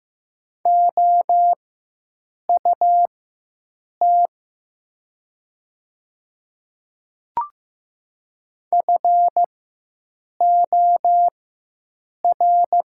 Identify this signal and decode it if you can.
Morse code, heard as a keyed tone, about 15 wpm, spaced as for 7 wpm, 700 Hz. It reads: OUT FOR
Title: 4x Character Spacing - Top 300 Words - 15wpm